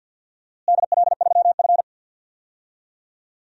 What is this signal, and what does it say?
Morse code, heard as a keyed tone, about 52 wpm, 700 Hz. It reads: DL3F
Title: DXCC-100 Call Signs - 4 Characters - Repeated 3x - 50wpm